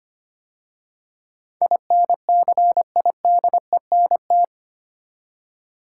Morse code, heard as a keyed tone, about 25 wpm, 690 Hz. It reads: INCIDENT
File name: 8 Letter Words - First 3 Letters Given - 25wpm